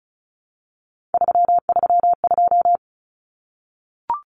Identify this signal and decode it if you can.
Morse code, heard as a keyed tone, about 34 wpm, 700 Hz. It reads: 332
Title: Numbers - 3-Digit 35wpm